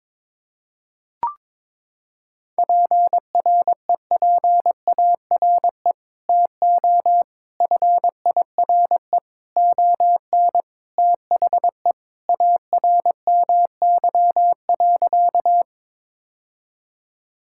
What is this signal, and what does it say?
Morse code, heard as a keyed tone, about 22 wpm, 700 Hz. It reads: PREPARE TO FIRE ON THE ARMY.
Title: Sentences from Top 1000 Words - Review Part 6 - 22wpm